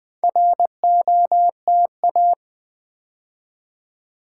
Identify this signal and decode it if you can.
Morse code, heard as a keyed tone, about 20 wpm, 700 Hz. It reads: ROTA
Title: Build A Word - 2 Letters At A Time - 8-Letter Words - 20wpm